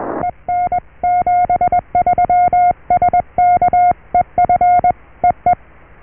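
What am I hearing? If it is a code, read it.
Morse code, heard as a keyed tone, about 21 wpm, 710 Hz. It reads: EN 73 SK EF EE
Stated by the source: Morse QSO on 14 MHz with a German station, received at RST 579.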